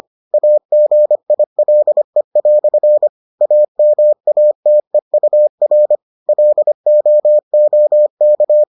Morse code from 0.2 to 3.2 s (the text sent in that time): AGILE"